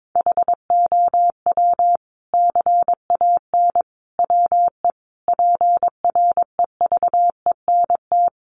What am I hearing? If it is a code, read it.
Morse code, heard as a keyed tone, about 22 wpm, 700 Hz. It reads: HOW CAN WE PREVENT
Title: Sentences from Top 1100 Words 22wpm